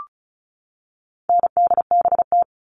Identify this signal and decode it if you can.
Morse code, heard as a keyed tone, about 34 wpm, 700 Hz. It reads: NDBT
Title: All Cut Numbers - 4-Digit 35wpm